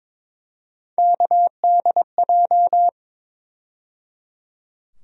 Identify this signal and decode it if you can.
Morse code, heard as a keyed tone, about 22 wpm, 700 Hz. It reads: KDJ
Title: USA Call Sign Postfixes - 3 Letters 22wpm